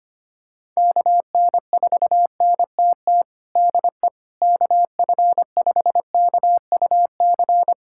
Morse code, heard as a keyed tone, about 25 wpm, 700 Hz. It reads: KN4NTT DE KF5KUC
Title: USA Callsign DE Callsign 25wpm